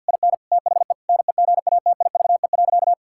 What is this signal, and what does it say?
Morse code, heard as a keyed tone, about 50 wpm, 700 Hz. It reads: IN THE NEGATIVE.
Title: Sentences from Top 1000 Words - Review Part 6 - 50wpm